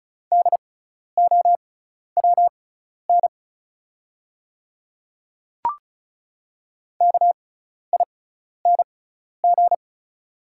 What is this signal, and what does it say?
Morse code, heard as a keyed tone, about 36 wpm, spaced as for 12 wpm, 700 Hz. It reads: DOWN KING
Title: 6x Character Spacing - Top 500 Words - 35wpm